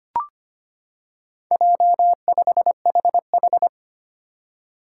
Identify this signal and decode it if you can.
Morse code, heard as a keyed tone, about 25 wpm, 700 Hz. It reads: J5HH
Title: Four Challenging Letter-Number Combinations 25wpm